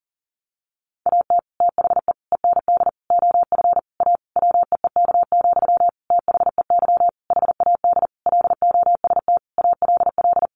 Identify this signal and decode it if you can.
Morse code, heard as a keyed tone, about 40 wpm, 700 Hz. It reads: AT THE END OF A WEEK, THEY HAD LOST ALL
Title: Sentences from Top 1400 Words 40wpm